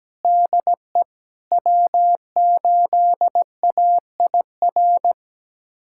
Morse code, heard as a keyed tone, about 17 wpm, 700 Hz. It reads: DE W8AIR
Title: USA Callsign DE Callsign 17wpm